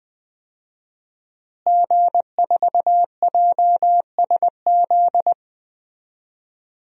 Morse code, heard as a keyed tone, about 20 wpm, 700 Hz. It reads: G4JSZ
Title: DXCC-100 Call Signs - 5 Characters - Repeated 3x - 20wpm